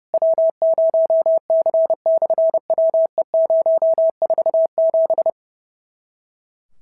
Morse code, heard as a keyed tone, about 30 wpm, 650 Hz. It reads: W0C/WE047
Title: SOTA QSO Practice 30wpm